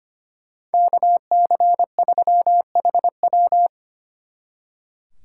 Morse code, heard as a keyed tone, about 31 wpm, 700 Hz. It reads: KC3HW
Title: Speed Racing - USA 2x2 Call Signs 25wpm